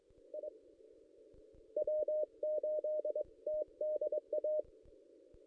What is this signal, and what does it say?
Morse code, heard as a keyed tone, about 22 wpm, 610 Hz. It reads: I W8TDA
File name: contests / ss-cw-k5tr-op-n5rz -1600